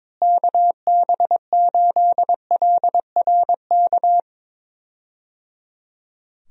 Morse code, heard as a keyed tone, about 22 wpm, 700 Hz. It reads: KB8LRK
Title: USA Call Signs - 6 Characters - Repeated 3x - 22wpm